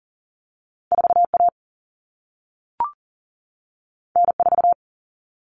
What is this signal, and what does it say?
Morse code, heard as a keyed tone, about 42 wpm, 700 Hz. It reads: 4A N4
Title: All Cut Numbers - 2-Digit 40wpm